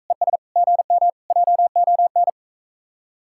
Morse code, heard as a keyed tone, about 45 wpm, 700 Hz. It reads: ES GM JON